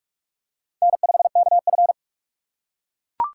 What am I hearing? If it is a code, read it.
Morse code, heard as a keyed tone, about 41 wpm, 690 Hz. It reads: N5KF
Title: USA Call Signs - 4x Character Spacing - 45wpm